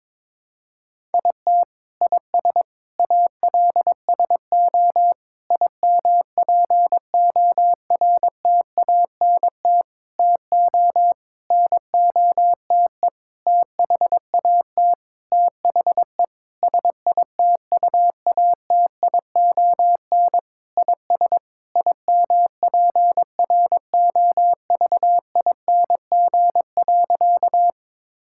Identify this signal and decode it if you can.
Morse code, heard as a keyed tone, about 22 wpm, 700 Hz. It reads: IT IS ALSO IMPORTANT TO NOTE THAT THE SITUATION IS IMPROVING.
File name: practice / 22wpm / Sentences with 8-Word Phrases - 22wpm